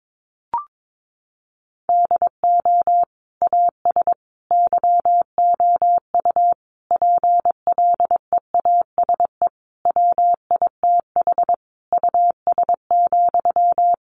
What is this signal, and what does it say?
Morse code, heard as a keyed tone, about 22 wpm, 700 Hz. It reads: DO AS YOU PLEASE WITH US,
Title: Sentences from Top 1500 Words 22wpm